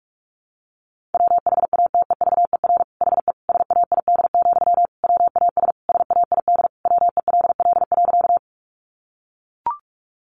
Morse code, heard as a keyed tone, about 44 wpm, 700 Hz. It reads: WHATEVER HE SAID, WAS SAID WELL.
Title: Sentences from Top 1100 Words 45wpm